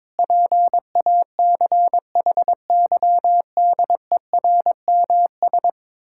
Morse code, heard as a keyed tone, about 22 wpm, 700 Hz. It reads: PACHYDERMS
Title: Pangram Challenge v2 - 22wpm